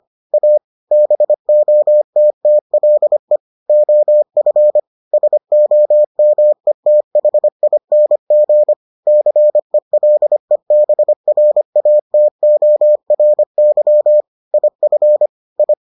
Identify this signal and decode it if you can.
Morse code, heard as a keyed tone, about 25 wpm, 600 Hz. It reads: A BOTTLE OF SOMETHING CELEBRATORY IF I